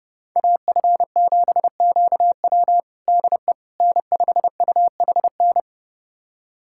Morse code, heard as a keyed tone, about 32 wpm, 700 Hz. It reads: AF7QW DE N5UHN
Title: USA Callsign DE Callsign - 5 Characters - 30wpm